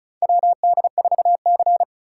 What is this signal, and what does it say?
Morse code, heard as a keyed tone, about 36 wpm, 700 Hz. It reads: WD4C
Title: USA Call Signs - 4 Characters - 6x Character Spacing - 35wpm